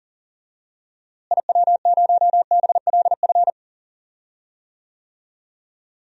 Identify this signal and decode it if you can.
Morse code, heard as a keyed tone, about 40 wpm, 700 Hz. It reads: IW0BLF
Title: DXCC-100 Call Signs - 6 Characters - Repeated 3x - 40wpm